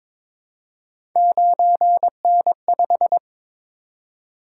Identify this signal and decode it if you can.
Morse code, heard as a keyed tone, about 22 wpm, 700 Hz. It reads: 9N5
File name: practice / 22wpm / Three Letter-Number Combinations - 22wpm